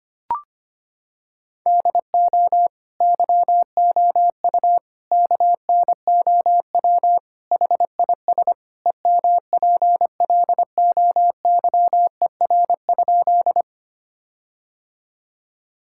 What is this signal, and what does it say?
Morse code, heard as a keyed tone, about 25 wpm, 700 Hz. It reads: DO YOU KNOW HIS EMPLOYER?
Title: Sentences from Top 1600 Words 25wpm